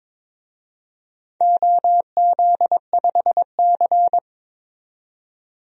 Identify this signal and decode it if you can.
Morse code, heard as a keyed tone, about 22 wpm, 700 Hz. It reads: OZ5C
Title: DXCC-100 Call Signs - 4 Characters - 22wpm